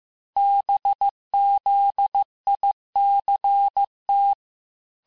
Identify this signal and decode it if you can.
Morse code, heard as a keyed tone, about 15 wpm, 800 Hz. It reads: BZICT
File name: cours-du-lundi files / cours-1585-texte-chiffre-vitesse-14mpm